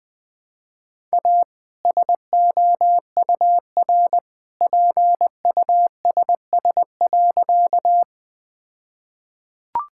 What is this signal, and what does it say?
Morse code, heard as a keyed tone, about 20 wpm, 700 Hz. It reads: A SOUR PUSS.
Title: All-Age Appropriate Jokes - 20wpm